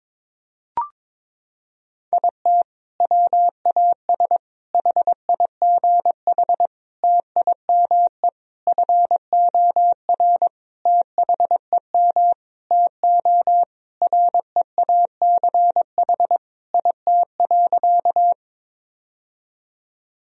Morse code, heard as a keyed tone, about 22 wpm, 700 Hz. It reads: IT WAS HIGH TIME FOR THEM TO REACH IT.
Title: Sentences from Top 1000 Words - Review Part 2 - 22wpm